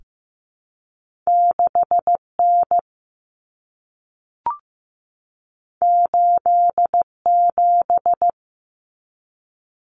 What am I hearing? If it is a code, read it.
Morse code, heard as a keyed tone, about 15 wpm, 700 Hz. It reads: 6N 87
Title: Common Cut Numbers - 2-Digit 15wpm